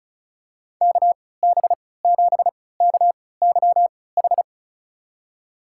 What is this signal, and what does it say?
Morse code, heard as a keyed tone, about 36 wpm, spaced as for 20 wpm, 700 Hz. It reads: KB7KYH
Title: USA Call Signs - 3x Character Spacing - 35wpm